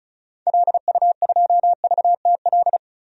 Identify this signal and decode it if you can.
Morse code, heard as a keyed tone, about 36 wpm, 700 Hz. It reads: LU2VTL
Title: DXCC-100 Call Signs - 6 Characters - 35wpm